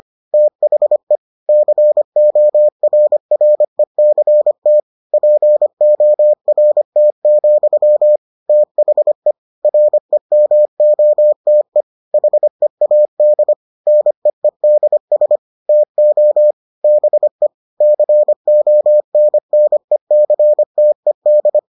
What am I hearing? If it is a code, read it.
Morse code, heard as a keyed tone, about 25 wpm, 600 Hz. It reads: THE CORRECT PORT, THE REMOTE HEAD NEEDS TO BE CONNECTED